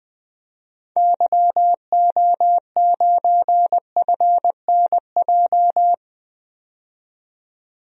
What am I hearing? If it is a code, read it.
Morse code, heard as a keyed tone, about 20 wpm, 700 Hz. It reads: YO9FNJ
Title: DXCC-100 Call Signs - 6 Characters - 20wpm